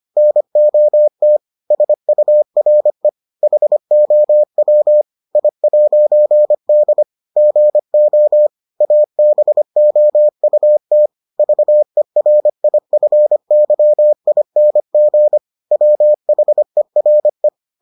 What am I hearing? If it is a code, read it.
Morse code, heard as a keyed tone, about 25 wpm, 600 Hz. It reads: NOT SURE HOW I'D GO ABOUT VERIFYING WHERE